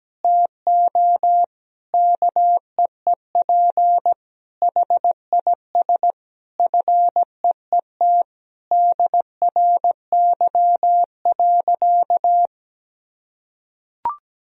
Morse code, heard as a keyed tone, about 17 wpm, 700 Hz. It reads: TO KEEP HIS FEET DRY.
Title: Sentences from Top 1000 Words - Review Part 3 - 17wpm